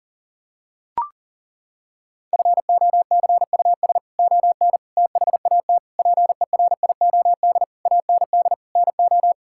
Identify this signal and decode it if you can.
Morse code, heard as a keyed tone, about 40 wpm, 700 Hz. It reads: FOCUS ON THAT PERIOD AND NO
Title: Sentences from Top 1000 Words - Review Part 5 - 40wpm